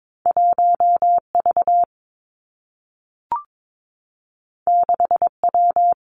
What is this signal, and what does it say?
Morse code, heard as a keyed tone, about 22 wpm, 700 Hz. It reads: 1 V 6 W
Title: Two Challenging Letter-Number Combinations 22wpm